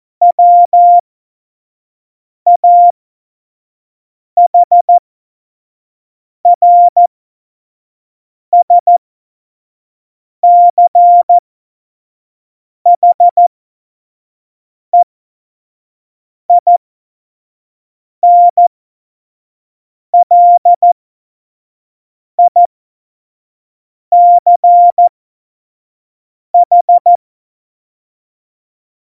Morse code, heard as a keyed tone, about 14 wpm, spaced as for 5 wpm, 700 Hz. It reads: WAHRSCHEINLICH
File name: Morsekurs / Witze / Witz 26. Januar 2021, 25 F70